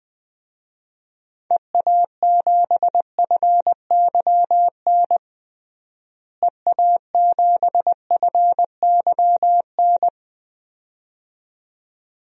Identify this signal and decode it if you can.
Morse code, heard as a keyed tone, about 21 wpm, 700 Hz. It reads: EA7FYN EA7FYN
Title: DXCC-100 Call Signs - 6 Characters - Repeated 2x - 20wpm